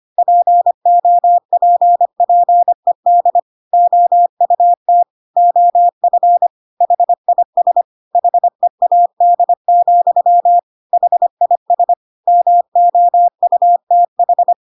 Morse code, heard as a keyed tone, about 25 wpm, 700 Hz. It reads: POPPED OUT OF HIS HEAD, HIS MOUTH
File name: Pinocchio / Pinocchio ch1 25 25 ik1hge